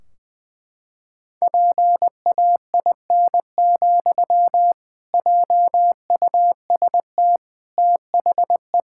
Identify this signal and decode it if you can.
Morse code, heard as a keyed tone, about 21 wpm, 700 Hz. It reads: PAIN, JUST THE
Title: Sentences from Top 1600 Words 20wpm